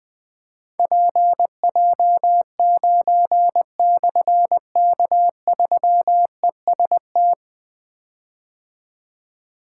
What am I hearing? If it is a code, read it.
Morse code, heard as a keyed tone, about 20 wpm, 700 Hz. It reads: PJ9/K3EST
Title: Difficult Call Signs 20wpm